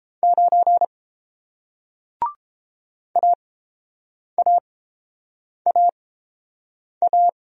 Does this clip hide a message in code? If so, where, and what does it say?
Yes: Morse code from 0.2 to 7.3 s, heard as a keyed tone, about 30 wpm, spaced as for 7 wpm, 700 Hz. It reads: 9 AAAA